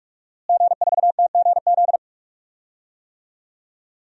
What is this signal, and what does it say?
Morse code, heard as a keyed tone, about 43 wpm, 700 Hz. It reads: G4TGZ
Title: DXCC-100 Call Signs - 5 Characters - Repeated 2x - 45wpm